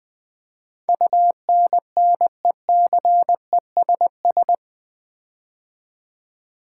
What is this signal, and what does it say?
Morse code, heard as a keyed tone, about 20 wpm, 700 Hz. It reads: UNNECESS
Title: Build A Word - 4 Letters At A Time - 11-Letter Words - 20wpm